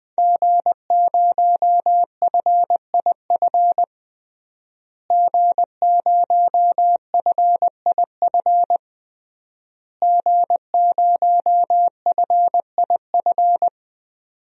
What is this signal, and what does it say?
Morse code, heard as a keyed tone, about 20 wpm, 700 Hz. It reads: G0FIF G0FIF G0FIF
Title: DXCC-100 Call Signs - 5 Characters - Repeated 3x - 20wpm